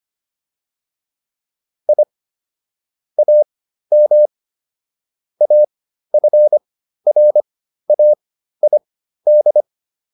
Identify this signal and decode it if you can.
Morse code, heard as a keyed tone, about 25 wpm, spaced as for 13 wpm, 600 Hz. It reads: I AM AFRAID